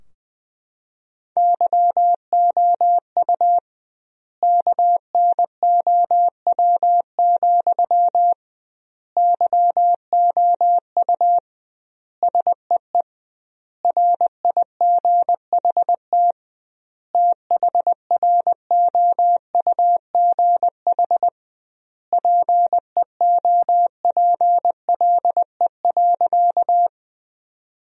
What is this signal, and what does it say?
Morse code, heard as a keyed tone, about 20 wpm, 700 Hz. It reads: YOU KNOW, YOU SEE RIGHT THROUGH PEOPLE.
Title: 2x Word-Spacing - Sentences from Top 200 Words - 20wpm